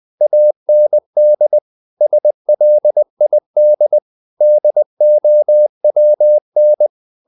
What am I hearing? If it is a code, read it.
Morse code, heard as a keyed tone, about 20 wpm, 600 Hz. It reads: AND SLID DOWN
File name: the old man and the sea part 2 20wpm